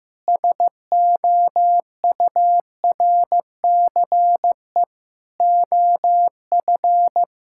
Morse code, heard as a keyed tone, about 16 wpm, 700 Hz. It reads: SOURCE OF
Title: Sentences from Top 1000 Words - Review Part 5 - 15wpm